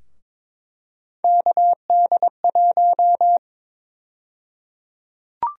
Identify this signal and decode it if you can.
Morse code, heard as a keyed tone, about 22 wpm, 700 Hz. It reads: KD1
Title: USA Call Sign Prefixes - 22wpm